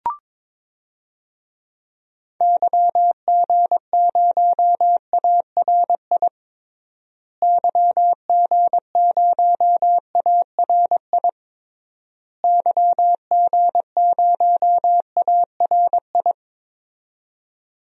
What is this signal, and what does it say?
Morse code, heard as a keyed tone, about 22 wpm, 700 Hz. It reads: YG0ARI YG0ARI YG0ARI
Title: DXCC-100 Call Signs - 6 Characters - Repeated 3x - 22wpm